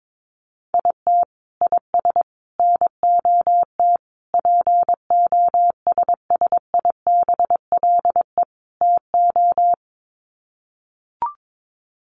Morse code, heard as a keyed tone, about 22 wpm, 700 Hz. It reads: IT IS NOT POSSIBLE TO
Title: Sets of 5 Words Encore2 22wpm